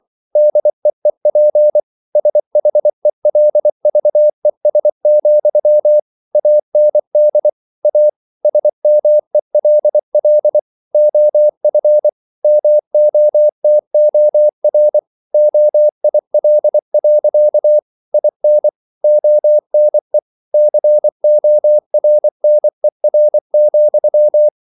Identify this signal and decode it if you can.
Morse code, heard as a keyed tone, about 24 wpm, 600 Hz. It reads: DEEP SHELVES, AND A SMELL OF MOTOR OIL. IN ONE CORNER,